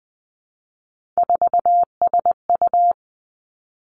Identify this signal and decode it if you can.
Morse code, heard as a keyed tone, about 20 wpm, 700 Hz. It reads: 4SU
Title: Three Challenging Letter-Number Combinations 20wpm